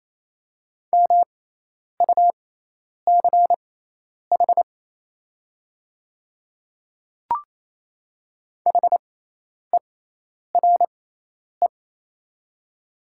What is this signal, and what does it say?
Morse code, heard as a keyed tone, about 28 wpm, spaced as for 10 wpm, 700 Hz. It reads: MUCH HERE
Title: 6x Character Spacing - Top 300 Words - 28wpm